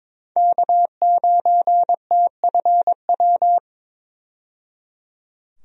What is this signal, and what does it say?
Morse code, heard as a keyed tone, about 22 wpm, 700 Hz. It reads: K9TFW
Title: USA Call Signs - 5 Characters - 22wpm